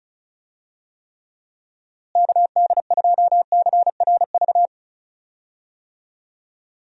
Morse code, heard as a keyed tone, about 36 wpm, 700 Hz. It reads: KD2CRV
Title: USA Call Signs - 6 Characters - Repeated 2x - 35wpm